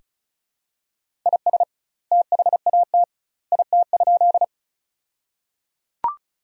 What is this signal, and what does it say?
Morse code, heard as a keyed tone, about 36 wpm, 700 Hz. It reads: IS THAT IT?
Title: 2x Word-Spacing - Sentences from Top 100 Words - 35wpm